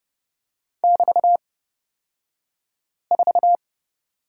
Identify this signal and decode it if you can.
Morse code, heard as a keyed tone, about 30 wpm, spaced as for 5 wpm, 700 Hz. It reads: <BT>4
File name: Lesson 36d - BT - Two Characters 30wpm